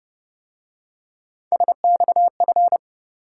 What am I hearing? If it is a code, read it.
Morse code, heard as a keyed tone, about 29 wpm, 700 Hz. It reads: SXF